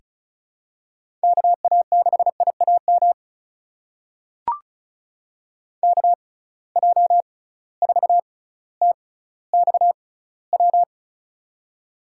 Morse code, heard as a keyed tone, about 34 wpm, spaced as for 18 wpm, 700 Hz. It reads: KA6IAM KJ4TXW
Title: USA Call Signs - 6 Characters - 6x Character Spacing - 35wpm